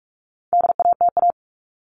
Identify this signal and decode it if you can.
Morse code, heard as a keyed tone, about 44 wpm, 700 Hz. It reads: DATA
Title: Top 501-750 Words 45wpm